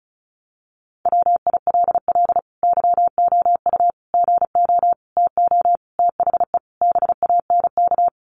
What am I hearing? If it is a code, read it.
Morse code, heard as a keyed tone, about 34 wpm, 700 Hz. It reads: WILL YOU GO TO THE BANK
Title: Sentences from Top 900 Words 35wpm